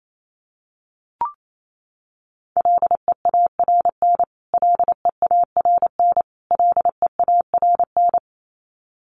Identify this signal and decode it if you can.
Morse code, heard as a keyed tone, about 28 wpm, 700 Hz. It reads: LEARN LEARN LEARN
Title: Top 300 Words Repeated 28wpm